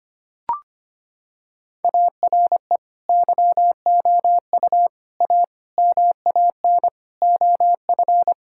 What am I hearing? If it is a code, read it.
Morse code, heard as a keyed tone, about 25 wpm, 700 Hz. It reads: ARE YOU A MAN OF
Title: Sentences from Top 1000 Words - Review Part 6 - 25wpm